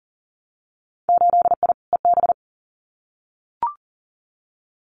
Morse code, heard as a keyed tone, about 42 wpm, 700 Hz. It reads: 8I EB
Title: Winter Field Day Exchanges Encore 40wpm